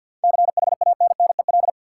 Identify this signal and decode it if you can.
Morse code, heard as a keyed tone, about 50 wpm, 700 Hz. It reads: CHANNEL
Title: Seven Letter Words - 50wpm